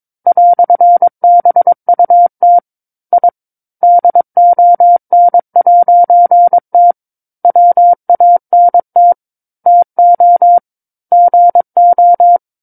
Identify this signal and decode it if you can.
Morse code, heard as a keyed tone, about 22 wpm, 700 Hz. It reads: "BUT I DON'T WANT TO GO
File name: 06-Chapter-06 22 WPM